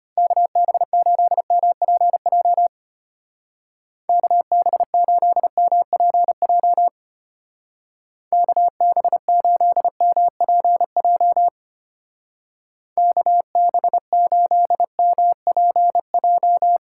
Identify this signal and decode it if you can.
Morse code, heard as a keyed tone, about 31 wpm, 700 Hz. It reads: KB8MPJ KB8MPJ KB8MPJ KB8MPJ
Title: Speed Racing - USA 2x3 Call Signs 25wpm